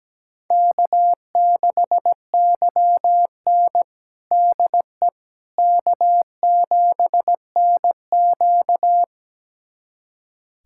Morse code, heard as a keyed tone, about 17 wpm, 700 Hz. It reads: K6YN DE K7NQ
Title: USA Callsign DE Callsign - 4 Characters - 17wpm